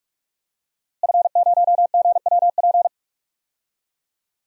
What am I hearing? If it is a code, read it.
Morse code, heard as a keyed tone, about 46 wpm, 700 Hz. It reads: F0GWP
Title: DXCC-100 Call Signs - 5 Characters - Repeated 2x - 45wpm